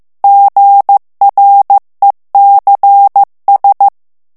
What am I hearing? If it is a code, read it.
Morse code, heard as a keyed tone, about 15 wpm, 800 Hz. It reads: GRECS
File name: FRANCAIS TEXTE CLAIR 2 VITESSE 14 MOTS MINUTES